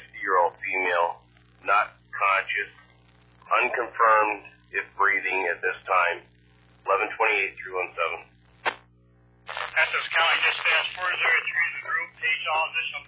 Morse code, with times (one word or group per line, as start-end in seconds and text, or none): none